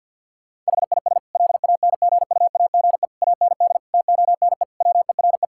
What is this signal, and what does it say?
Morse code, heard as a keyed tone, about 50 wpm, 700 Hz. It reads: HIS LANGUAGE AND TONE WERE
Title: Sentences from Top 1600 Words 50wpm